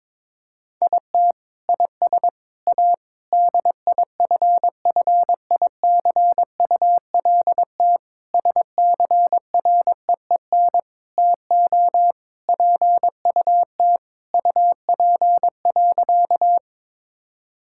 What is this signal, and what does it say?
Morse code, heard as a keyed tone, about 21 wpm, 700 Hz. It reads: IT IS A DIFFICULT SCREEN TO PUT UP.
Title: Sentences from Top 1000 Words - Review Part 6 - 22wpm